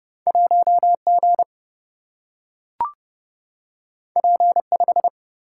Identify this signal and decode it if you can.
Morse code, heard as a keyed tone, about 30 wpm, 700 Hz. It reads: 1 G P 5
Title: Single-Letter Single-Number 30wpm